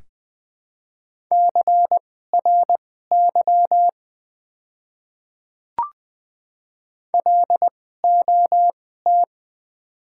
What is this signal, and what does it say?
Morse code, heard as a keyed tone, about 20 wpm, spaced as for 15 wpm, 700 Hz. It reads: CRY LOT
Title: 2x Character Spacing - Top 500 Words - 20wpm